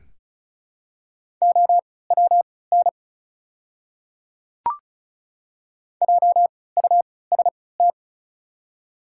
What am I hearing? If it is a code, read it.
Morse code, heard as a keyed tone, about 36 wpm, spaced as for 20 wpm, 700 Hz. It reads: OWN JUST
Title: 3x Character Spacing - Top 300 Words - 35wpm